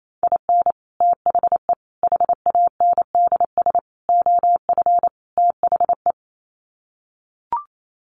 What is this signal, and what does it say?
Morse code, heard as a keyed tone, about 28 wpm, 700 Hz. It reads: IN THE HANDS OF THE
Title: Sets of 5 Words 28wpm